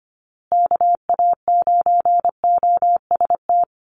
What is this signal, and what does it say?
Morse code, heard as a keyed tone, about 25 wpm, 700 Hz. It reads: KA9OST
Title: USA Call Signs Repeated 2x 25wpm